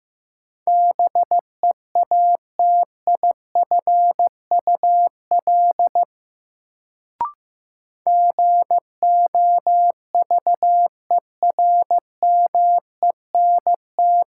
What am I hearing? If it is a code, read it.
Morse code, heard as a keyed tone, about 15 wpm, 700 Hz. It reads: BEATIFUL GOVERMENT
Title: Sentences from Top 1300 Words - New Word Drill - 15wpm